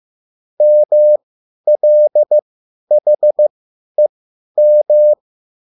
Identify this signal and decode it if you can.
Morse code, heard as a keyed tone, about 15 wpm, spaced as for 10 wpm, 600 Hz. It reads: MLHEM